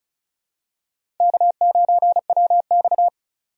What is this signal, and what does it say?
Morse code, heard as a keyed tone, about 35 wpm, 700 Hz. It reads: K9WX
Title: USA Call Signs - 4 Characters - 35wpm